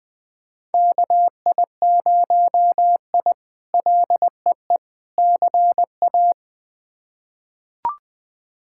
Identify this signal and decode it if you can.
Morse code, heard as a keyed tone, about 21 wpm, 700 Hz. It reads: KI0I LEE CA
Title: K1USN SST QSO Practice - 20wpm